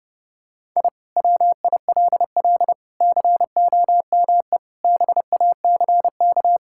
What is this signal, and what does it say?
Morse code, heard as a keyed tone, about 30 wpm, 700 Hz. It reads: I WILL COME BACK
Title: Sentences from Top 1000 Words - Review Part 6 - 30wpm